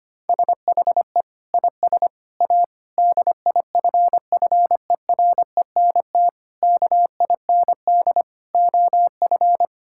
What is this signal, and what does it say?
Morse code, heard as a keyed tone, about 25 wpm, 700 Hz. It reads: SHE IS A DIFFERENT KIND OF
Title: Sentences from Top 1000 Words - Review Part 5 - 25wpm